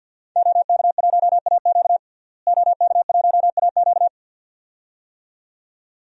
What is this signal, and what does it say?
Morse code, heard as a keyed tone, about 52 wpm, 700 Hz. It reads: OK1AQ OK1AQ
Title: DXCC-100 Call Signs - 5 Characters - Repeated 2x - 50wpm